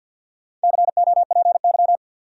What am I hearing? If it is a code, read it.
Morse code, heard as a keyed tone, about 53 wpm, 700 Hz. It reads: COPY
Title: Sentences from Top 1400 Words - New Word Drill - 50wpm